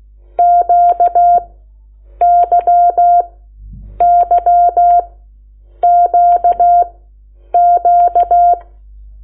Morse code, heard as a keyed tone, about 16 wpm, spaced as for 8 wpm, 700 Hz. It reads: QYYQQ
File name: Lecciones / CLASE Nº8